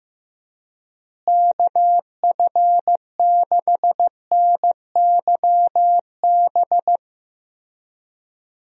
Morse code, heard as a keyed tone, about 15 wpm, 700 Hz. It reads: KF6NYB
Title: USA Call Signs - 6 Characters - Repeated 3x - 15wpm